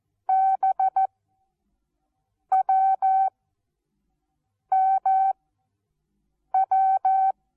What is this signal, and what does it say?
Morse code, heard as a keyed tone, about 14 wpm, 770 Hz. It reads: B W M W
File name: Onlinelehrgang / morsen / DJ4UF Morsen19